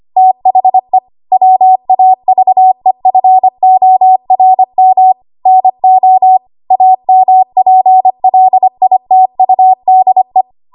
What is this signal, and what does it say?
Morse code, heard as a keyed tone, about 25 wpm, 750 Hz. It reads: THE WAVEFORM NO AMPLITUDE
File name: Archive / 25 WPM / 25WPM